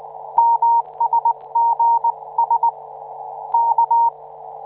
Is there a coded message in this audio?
Yes: Morse code from 0.4 to 4.1 s, heard as a keyed tone, about 19 wpm, 930 Hz. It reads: MSGS K